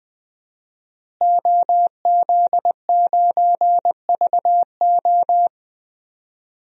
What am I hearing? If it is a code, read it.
Morse code, heard as a keyed tone, about 21 wpm, 700 Hz. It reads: OZ9VO
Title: DXCC-100 Call Signs - 5 Characters - Repeated 2x - 20wpm